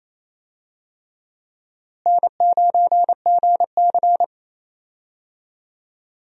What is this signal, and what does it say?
Morse code, heard as a keyed tone, about 28 wpm, 700 Hz. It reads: N9GC
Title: USA Call Signs - 4 Characters - Repeated 3x - 28wpm